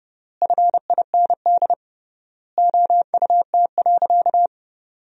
Morse code, heard as a keyed tone, about 30 wpm, 700 Hz. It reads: FIND OUT.
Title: 3x Word-Spacing - Sentences from Top 500 Words - 30wpm